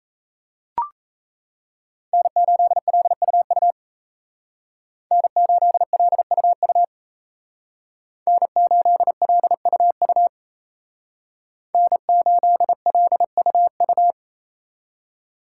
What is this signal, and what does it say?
Morse code, heard as a keyed tone, about 36 wpm, 700 Hz. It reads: N8LUU N8LUU N8LUU N8LUU